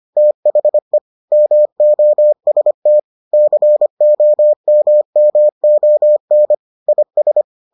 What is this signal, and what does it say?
Morse code, heard as a keyed tone, about 25 wpm, 600 Hz. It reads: THE MOST COMMON IS